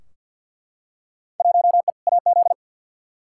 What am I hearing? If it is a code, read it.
Morse code, heard as a keyed tone, about 48 wpm, 700 Hz. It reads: 1E AZ
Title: Field Day Exchanges - 50wpm